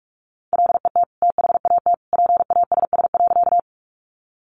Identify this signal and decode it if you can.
Morse code, heard as a keyed tone, about 44 wpm, 700 Hz. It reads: LET THAT PASS.
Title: Sentences from Top 400 Words 45wpm